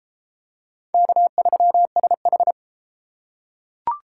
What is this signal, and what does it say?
Morse code, heard as a keyed tone, about 30 wpm, 700 Hz. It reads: K3SH